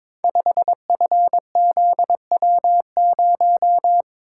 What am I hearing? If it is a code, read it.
Morse code, heard as a keyed tone, about 22 wpm, 690 Hz. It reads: 5FZW0
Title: Code Groups - 5 Letter:Numbers - 22wpm